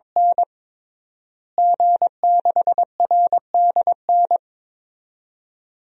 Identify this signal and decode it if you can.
Morse code, heard as a keyed tone, about 22 wpm, 700 Hz. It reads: N G6RDN
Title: DXCC-100 Call Signs - 5 Characters - Repeated 2x - 22wpm